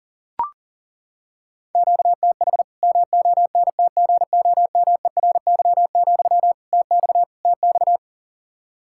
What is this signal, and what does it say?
Morse code, heard as a keyed tone, about 40 wpm, 700 Hz. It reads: QTH MONTGOMERY, TX TX
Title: Practice QSO - Exchange 1 - Beginning - 40wpm